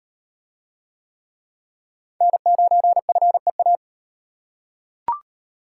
Morse code, heard as a keyed tone, about 34 wpm, 700 Hz. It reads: N9FEA